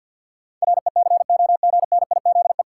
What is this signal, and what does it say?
Morse code, heard as a keyed tone, about 46 wpm, 700 Hz. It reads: RECOGNIZE
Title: Sentences from Top 1400 Words - New Word Drill - 50wpm